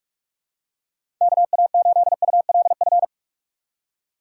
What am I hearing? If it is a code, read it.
Morse code, heard as a keyed tone, about 46 wpm, 700 Hz. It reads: KA8ULF
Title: USA Call Signs - 6 Characters - 3x Character Spacing - 45wpm